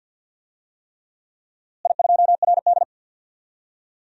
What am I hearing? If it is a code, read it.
Morse code, heard as a keyed tone, about 46 wpm, 700 Hz. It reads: I2RD
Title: DXCC-100 Call Signs - 4 Characters - Repeated 3x - 50wpm